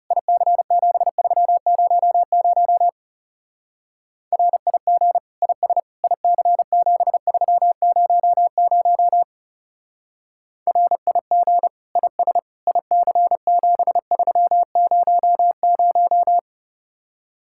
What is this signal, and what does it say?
Morse code, heard as a keyed tone, about 34 wpm, 700 Hz. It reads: IC7300 RIG IS IC7300 RIG IS IC7300
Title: Speed-Racing Common QSO Phrases - 30wpm